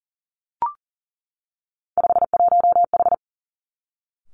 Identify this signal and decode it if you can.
Morse code, heard as a keyed tone, about 41 wpm, 700 Hz. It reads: 51H
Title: Three Challenging Letter-Number Combinations 40wpm